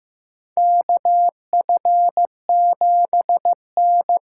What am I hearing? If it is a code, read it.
Morse code, heard as a keyed tone, about 15 wpm, 700 Hz. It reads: KF7N
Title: USA Call Signs - 6 Characters - 6x Character Spacing - 15wpm